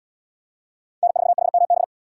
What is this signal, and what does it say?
Morse code, heard as a keyed tone, about 65 wpm, 700 Hz. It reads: N4HMB